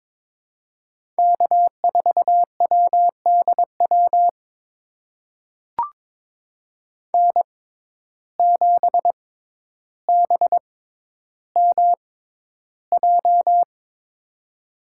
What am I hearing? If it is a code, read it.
Morse code, heard as a keyed tone, about 22 wpm, spaced as for 11 wpm, 700 Hz. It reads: K4WDW N7BMJ
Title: USA Call Signs - 5 Characters - 6x Character Spacing - 22wpm